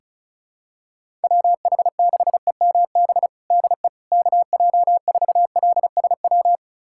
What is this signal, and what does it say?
Morse code, heard as a keyed tone, about 36 wpm, 700 Hz. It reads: WH6EMB DE KJ4LSW
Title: USA Callsign DE Callsign 35wpm